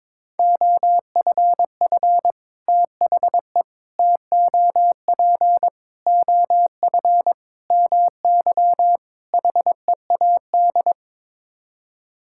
Morse code, heard as a keyed tone, about 22 wpm, 700 Hz. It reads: OFF THE TOP OF MY HEAD
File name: practice / 22wpm / Sets of 6 Words - 22wpm